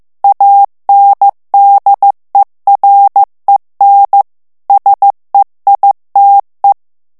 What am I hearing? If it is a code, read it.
Morse code, heard as a keyed tone, about 15 wpm, 800 Hz. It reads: ANDEREN SEITE
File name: morse mp3 / ALLEMAND texte clair 1 vitesse 14 mots minute